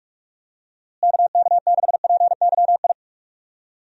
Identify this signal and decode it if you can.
Morse code, heard as a keyed tone, about 45 wpm, 700 Hz. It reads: KK6PYI
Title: USA Call Signs - 6 Characters - Repeated 3x - 45wpm